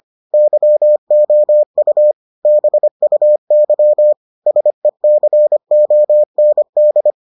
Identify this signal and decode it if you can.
Morse code, heard as a keyed tone, about 25 wpm, 600 Hz. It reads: YOU BUY SECOND